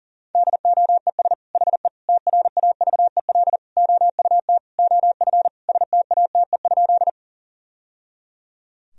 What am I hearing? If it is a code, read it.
Morse code, heard as a keyed tone, about 40 wpm, 700 Hz. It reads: DOES HE TRAVEL OUT OF STATE?
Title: Sentences from Top 1000 Words - Review Part 2 - 40wpm